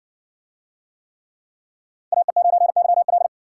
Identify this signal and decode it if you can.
Morse code, heard as a keyed tone, about 60 wpm, 700 Hz. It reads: AE9YL